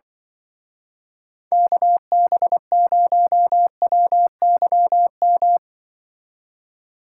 Morse code, heard as a keyed tone, about 24 wpm, 700 Hz. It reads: KB0WYM
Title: Speed Racing - USA 2x3 Call Signs 20wpm